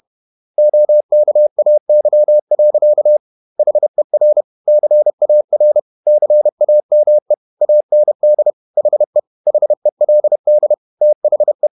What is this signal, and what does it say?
Morse code, heard as a keyed tone, about 31 wpm, 600 Hz. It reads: OKAY. HER CAR CAME AND HE HELD THE
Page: Never 31wpm | CW med Gnister